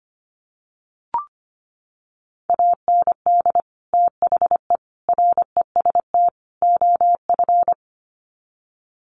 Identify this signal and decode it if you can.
Morse code, heard as a keyed tone, about 25 wpm, 700 Hz. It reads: AND THE REST OF
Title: Sets of 4 Words 25wpm